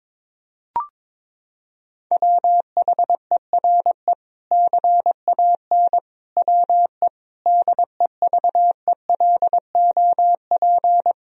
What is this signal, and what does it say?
Morse code, heard as a keyed tone, about 22 wpm, 700 Hz. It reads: WHERE CAN WE DEVELOP
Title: Sentences from Top 1000 Words - Review Part 2 - 22wpm